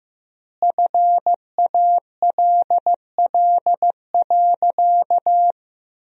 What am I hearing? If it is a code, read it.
Morse code, heard as a keyed tone, about 15 wpm, 700 Hz. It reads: FALL.
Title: 4x Word-Spacing - Sentences from Top 500 Words - 15wpm